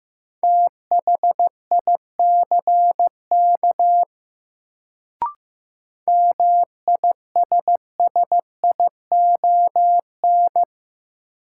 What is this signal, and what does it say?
Morse code, heard as a keyed tone, about 15 wpm, 700 Hz. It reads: THICK MISSION
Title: Sentences from Top 1600 Words - New Word Drill Alt - 15wpm